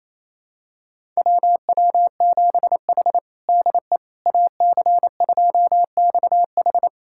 Morse code, heard as a keyed tone, about 28 wpm, 700 Hz. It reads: WW7H DE AC2XH
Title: USA Callsign DE Callsign 28wpm